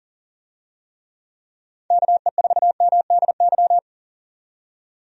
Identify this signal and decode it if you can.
Morse code, heard as a keyed tone, about 40 wpm, 700 Hz. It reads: KE4MDY
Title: USA Call Signs - 6 Characters - 40wpm